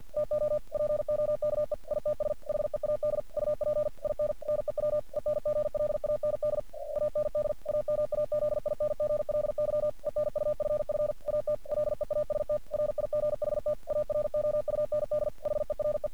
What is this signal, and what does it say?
Morse code, heard as a keyed tone, about 46 wpm, 600 Hz. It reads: TO POKE ITS HEAD UP IN NEW ENGLAND AND AMAZINGLY ENUFF AT LEAST RIGHT AROUND HERE
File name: archived / CRSnet